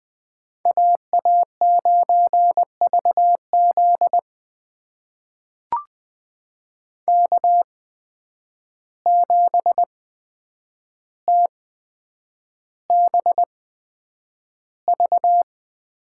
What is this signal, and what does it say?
Morse code, heard as a keyed tone, about 20 wpm, spaced as for 9 wpm, 700 Hz. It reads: AA9VZ K7TBV
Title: USA Call Signs - 5 Characters - 8x Character Spacing - 20wpm